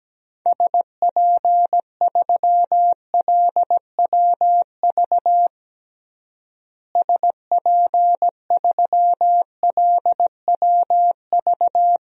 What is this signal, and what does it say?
Morse code, heard as a keyed tone, about 17 wpm, 700 Hz. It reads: SP3LWV SP3LWV
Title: DXCC-100 Call Signs - 6 Characters - Repeated 2x - 17wpm